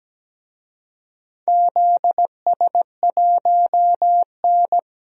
Morse code, heard as a keyed tone, about 17 wpm, 700 Hz. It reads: ZS1N
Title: DXCC-100 Call Signs - 4 Characters - 17wpm